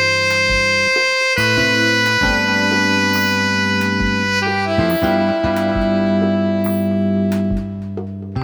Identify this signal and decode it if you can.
Dial-up modem handshake.